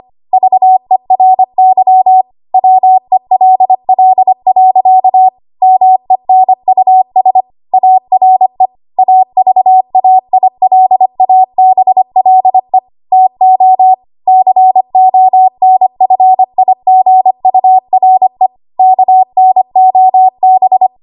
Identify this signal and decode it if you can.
Morse code, heard as a keyed tone, about 25 wpm, 750 Hz. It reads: VERY WELL. MENUS ARE AVAILABLE TO CONFIGURE KNOB
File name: Archive / 25 WPM / 25WPM